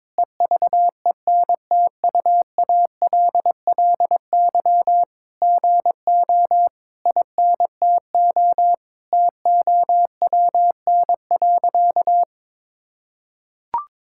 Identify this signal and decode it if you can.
Morse code, heard as a keyed tone, about 22 wpm, 700 Hz. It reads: EVENTUALLY GO INTO TOWN.
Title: Sentences from Top 1000 Words - Review Part 6 - 22wpm